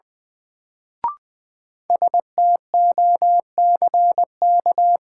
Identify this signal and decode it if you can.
Morse code, heard as a keyed tone, about 21 wpm, 700 Hz. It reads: STOCK
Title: Sets of 5 Words - New Word Drill - 20wpm